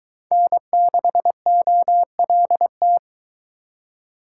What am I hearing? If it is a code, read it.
Morse code, heard as a keyed tone, about 22 wpm, 700 Hz. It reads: N6OLT